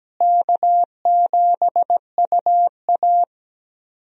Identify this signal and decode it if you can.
Morse code, heard as a keyed tone, about 17 wpm, 700 Hz. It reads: K7UA
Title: USA Call Signs - 4 Characters - Repeated 3x - 17wpm